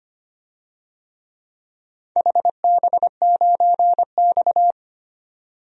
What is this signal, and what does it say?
Morse code, heard as a keyed tone, about 25 wpm, 700 Hz. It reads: HB9X
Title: DXCC-100 Call Signs - 4 Characters - 25wpm